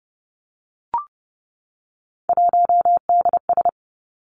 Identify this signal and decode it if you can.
Morse code, heard as a keyed tone, about 30 wpm, 700 Hz. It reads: 1DS
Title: Three Challenging Letter-Number Combinations 30wpm